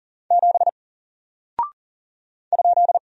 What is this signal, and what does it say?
Morse code, heard as a keyed tone, about 39 wpm, 700 Hz. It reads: 7 ?
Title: Single Character (Letter:Number:Punctuation:Prosign) 40wpm